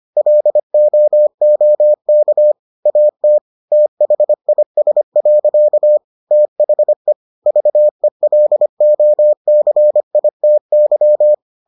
Morse code, heard as a keyed tone, about 25 wpm, 600 Hz. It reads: LOOK AT THIS. THE VELOCITY